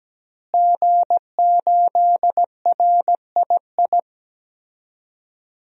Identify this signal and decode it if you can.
Morse code, heard as a keyed tone, about 17 wpm, 700 Hz. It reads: G8RII
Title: DXCC-100 Call Signs - 5 Characters - Repeated 3x - 17wpm